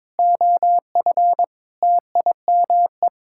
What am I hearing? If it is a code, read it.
Morse code, heard as a keyed tone, about 22 wpm, 700 Hz. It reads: OF TIME
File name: practice / 22wpm / Sets of 6 Words - 22wpm